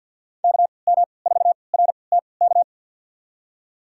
Morse code, heard as a keyed tone, about 49 wpm, spaced as for 28 wpm, 700 Hz. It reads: KM4RTK
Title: USA Call Signs - 3x Character Spacing - 50wpm